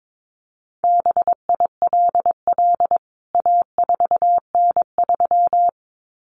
Morse code, heard as a keyed tone, about 22 wpm, 700 Hz. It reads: BILL A4N3
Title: CWT Exchange 22wpm